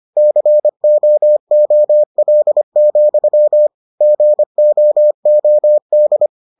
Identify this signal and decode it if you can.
Morse code, heard as a keyed tone, about 25 wpm, 600 Hz. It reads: COOL, GOOD